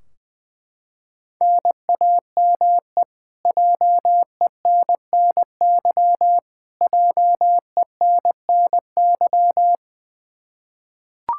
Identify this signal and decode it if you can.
Morse code, heard as a keyed tone, about 20 wpm, 700 Hz. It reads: NAME JENNY JENNY
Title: Practice QSO - Exchange 1 - Beginning - 20wpm